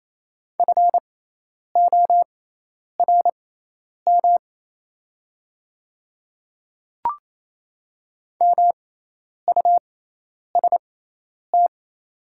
Morse code, heard as a keyed tone, about 28 wpm, spaced as for 10 wpm, 700 Hz. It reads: FORM MUST
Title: 6x Character Spacing - Top 200 Words - 28wpm